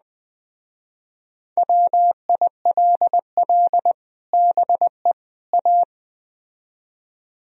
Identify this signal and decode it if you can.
Morse code, heard as a keyed tone, about 20 wpm, 700 Hz. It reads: WILL BE A
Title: Speed-Racing - Sets of 3 Words 17wpm